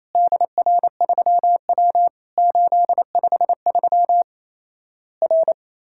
Morse code, heard as a keyed tone, about 28 wpm, 700 Hz. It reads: DR3W 853 R
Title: Worked All Europe DX Contest - QTC Exchange - 28wpm